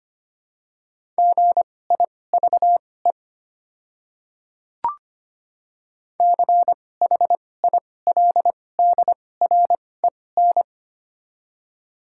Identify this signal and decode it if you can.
Morse code, heard as a keyed tone, about 25 wpm, spaced as for 18 wpm, 700 Hz. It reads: GIVE CHILDREN
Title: 2x Character Spacing - Top 300 Words - 25wpm